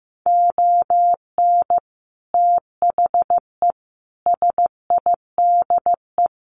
Morse code, heard as a keyed tone, about 15 wpm, 700 Hz. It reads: ON THE SIDE
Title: Sets of 5 Words 15wpm